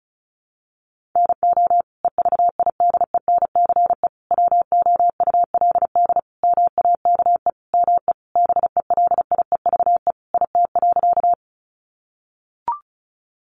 Morse code, heard as a keyed tone, about 34 wpm, 700 Hz. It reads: NO EVIDENCE WOULD MAKE ME BELIEVE IT.
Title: Sentences from Top 1300 Words 35wpm